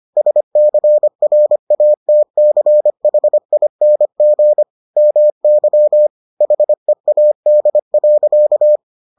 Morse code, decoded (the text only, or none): SCRATCHING MY HEAD.